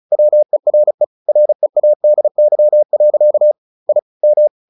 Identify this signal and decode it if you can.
Morse code, heard as a keyed tone, about 34 wpm, 600 Hz. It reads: WERE READY. I M
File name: the old man and the sea / the old man and the sea 1 35wpm